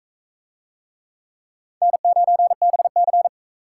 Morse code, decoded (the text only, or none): N9BC